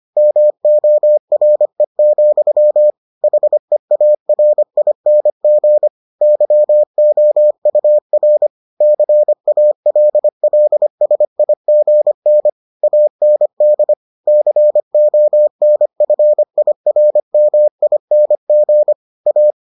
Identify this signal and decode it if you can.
Morse code, heard as a keyed tone, about 25 wpm, 600 Hz. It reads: MORE, HEARING YOUR CALLSIGN AND CONFIRMING A